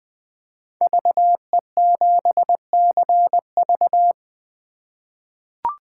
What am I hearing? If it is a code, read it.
Morse code, heard as a keyed tone, about 20 wpm, 700 Hz. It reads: VE7CV
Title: 2023 SOTA Chaser Call Signs - 20wpm